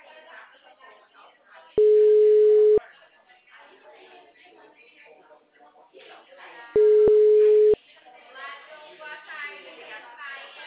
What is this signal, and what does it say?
Ringback tone (Europe, 425 Hz), with speech.